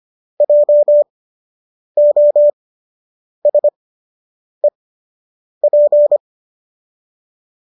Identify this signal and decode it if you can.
Morse code, heard as a keyed tone, about 25 wpm, spaced as for 8 wpm, 590 Hz. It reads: JOSEP